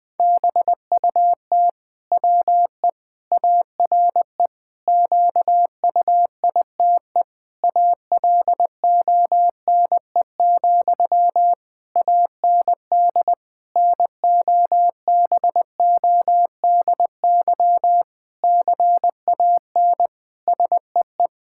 Morse code, heard as a keyed tone, about 21 wpm, 700 Hz. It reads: BUT WE ARE QUITE ALONE, AND NOBODY CAN SEE
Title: Sentences from Top 1600 Words 20wpm